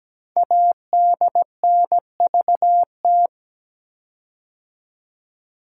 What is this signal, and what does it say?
Morse code, heard as a keyed tone, about 17 wpm, 700 Hz. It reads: ADNVT
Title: Code Groups - 5 Letters - 17wpm